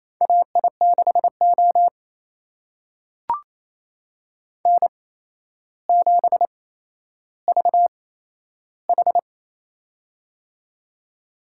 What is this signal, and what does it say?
Morse code, heard as a keyed tone, about 27 wpm, spaced as for 12 wpm, 700 Hz. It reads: AI6O N7VH
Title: USA Call Signs - 4 Characters - 8x Character Spacing - 28wpm